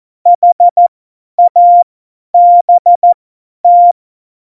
Morse code, heard as a keyed tone, about 14 wpm, spaced as for 10 wpm, 700 Hz. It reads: HABT
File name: Morsekurs / Witze / Witz 04. Juni 2019, 50 F70